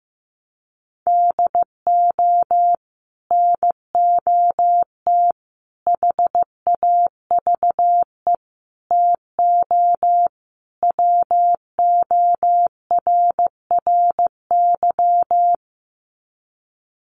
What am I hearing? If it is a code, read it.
Morse code, heard as a keyed tone, about 15 wpm, 700 Hz. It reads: DO NOT HAVE TO WORRY
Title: Sets of 5 Words 15wpm